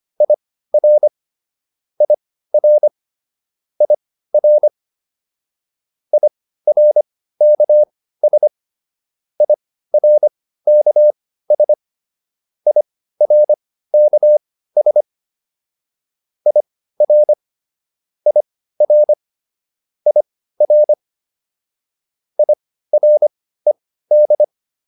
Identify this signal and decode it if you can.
Morse code, heard as a keyed tone, about 25 wpm, spaced as for 15 wpm, 600 Hz. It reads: IR IR IR IRKS IRKS IRKS IR IR IR IRED